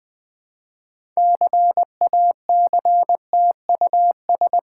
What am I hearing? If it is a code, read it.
Morse code, heard as a keyed tone, about 20 wpm, 690 Hz. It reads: CACTUS
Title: Difficult Words with Hints - 20wpm